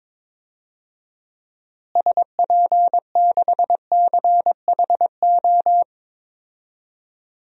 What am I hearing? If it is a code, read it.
Morse code, heard as a keyed tone, about 22 wpm, 700 Hz. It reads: SP6CHO
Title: DXCC-100 Call Signs - 6 Characters - Repeated 3x - 22wpm